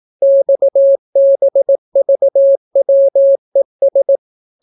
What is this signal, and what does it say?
Morse code, heard as a keyed tone, about 18 wpm, 550 Hz. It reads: XBVWES